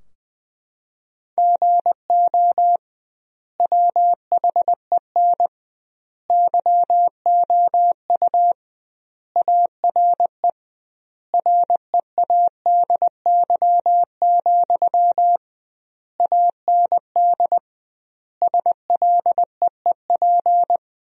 Morse code, heard as a keyed tone, about 20 wpm, 700 Hz. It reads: GO WHEN YOU ARE READY, AND SLEEP
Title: 2x Word-Spacing - Sentences from Top 500 Words - 20wpm